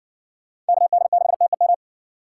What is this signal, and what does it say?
Morse code, heard as a keyed tone, about 56 wpm, 700 Hz. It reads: KD6TEM